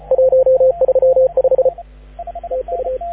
Morse code, heard as a keyed tone, about 31 wpm, 530 Hz, with radio noise. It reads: 135 TU
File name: AUDIO2019 / HG6N CW SSB MIX